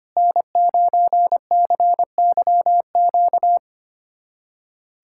Morse code, heard as a keyed tone, about 25 wpm, 700 Hz. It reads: N9CYQ
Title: USA Callsign DE Callsign - 5 Characters - 25wpm